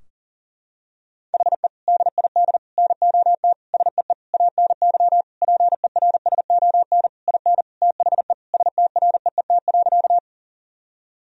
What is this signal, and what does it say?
Morse code, heard as a keyed tone, about 40 wpm, 700 Hz. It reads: HE DID NOT SEE ANY PERSON IN THE STREET.
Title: Sentences from Top 1000 Words - Review Part 3 - 40wpm